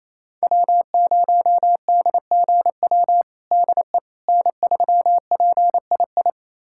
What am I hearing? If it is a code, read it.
Morse code, heard as a keyed tone, about 28 wpm, 700 Hz. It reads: W0DGW DE N3PII
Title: USA Callsign DE Callsign - 5 Characters - 28wpm